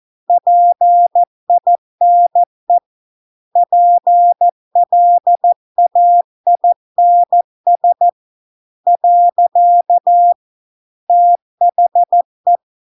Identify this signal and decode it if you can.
Morse code, heard as a keyed tone, about 14 wpm, 700 Hz. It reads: PINE PLAINS . THE